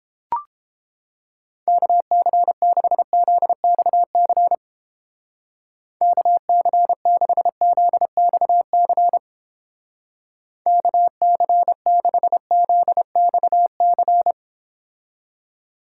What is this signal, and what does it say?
Morse code, heard as a keyed tone, about 30 wpm, 700 Hz. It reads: KC6ZXC KC6ZXC KC6ZXC